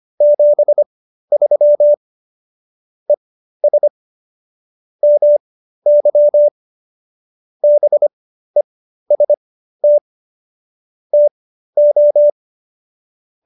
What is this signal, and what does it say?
Morse code, heard as a keyed tone, about 25 wpm, spaced as for 13 wpm, 590 Hz. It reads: 73 ES MY BEST TO